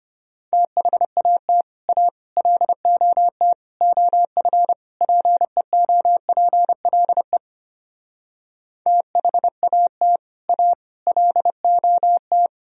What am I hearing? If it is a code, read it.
Morse code, heard as a keyed tone, about 33 wpm, 700 Hz. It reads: THAT A LOT OF PEOPLE THAT A LOT